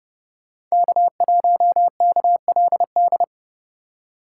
Morse code, heard as a keyed tone, about 38 wpm, 700 Hz. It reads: K1KLD